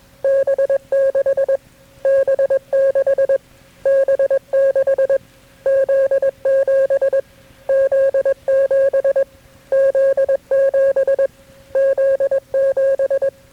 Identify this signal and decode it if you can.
Morse code, heard as a keyed tone, about 21 wpm, 540 Hz. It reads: B6 B6 B6 Z7 Z7 Z7 Z7
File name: KB6MT-tape3-side1